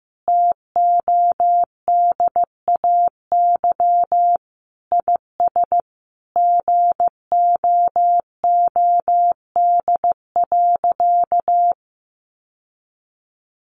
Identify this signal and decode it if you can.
Morse code, heard as a keyed tone, about 15 wpm, 700 Hz. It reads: TODAY IS GOOD.
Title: Sentences from Top 1400 Words 15wpm